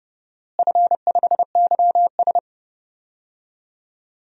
Morse code, heard as a keyed tone, about 30 wpm, 700 Hz. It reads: F5YS
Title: DXCC-100 Call Signs - 4 Characters - Repeated 2x - 30wpm